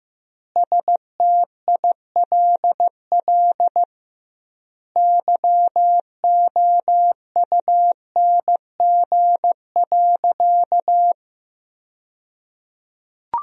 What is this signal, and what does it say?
Morse code, heard as a keyed tone, about 15 wpm, 700 Hz. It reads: STILL YOUNG.
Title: 2x Word-Spacing - Sentences from Top 400 Words - 15wpm